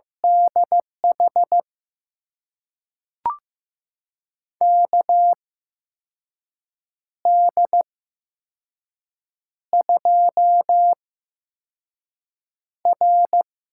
Morse code, heard as a keyed tone, about 15 wpm, spaced as for 5 wpm, 700 Hz. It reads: DH KD2R
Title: USA Call Signs - 8x Character Spacing - 15wpm